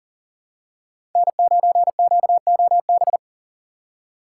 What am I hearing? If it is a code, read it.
Morse code, heard as a keyed tone, about 40 wpm, 700 Hz. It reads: N9QOB
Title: USA Call Signs - 5 Characters - 4x Character Spacing - 40wpm